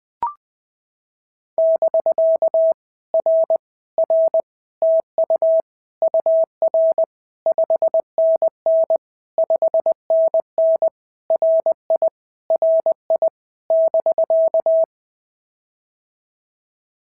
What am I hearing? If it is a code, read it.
Morse code, heard as a keyed tone, about 21 wpm, 650 Hz. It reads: <BK> R R TU UR 5NN 5NN RI RI <BK>
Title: POTA QSO Practice - 20wpm